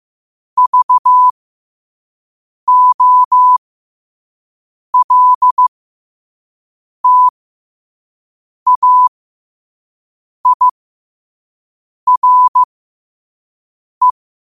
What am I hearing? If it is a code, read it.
Morse code, heard as a keyed tone, about 15 wpm, spaced as for 5 wpm, 1000 Hz. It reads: VOLTAIRE
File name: qotdfiles / 5wpm